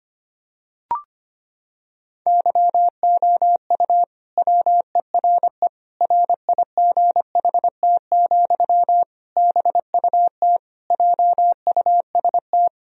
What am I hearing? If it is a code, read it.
Morse code, heard as a keyed tone, about 25 wpm, 700 Hz. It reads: YOU WERE RIGHT, BUT JUST
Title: Sentences from Top 1000 Words - Review Part 2 - 25wpm